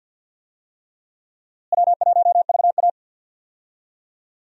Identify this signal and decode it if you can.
Morse code, heard as a keyed tone, about 52 wpm, 700 Hz. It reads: W1VA
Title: USA Call Signs - 4 Characters - Repeated 3x - 50wpm